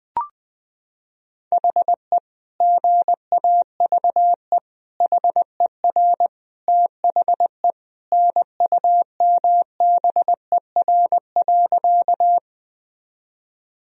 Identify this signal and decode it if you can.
Morse code, heard as a keyed tone, about 21 wpm, 700 Hz. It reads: HE GAVE HER THE NUMBER.
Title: Sentences from Top 1000 Words - Review Part 1 - 20wpm